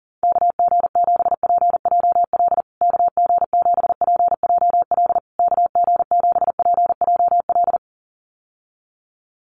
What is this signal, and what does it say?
Morse code, heard as a keyed tone, about 40 wpm, 700 Hz. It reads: KG7PJL KG7PJL KG7PJL
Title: USA Call Signs Repeated 3x 40wpm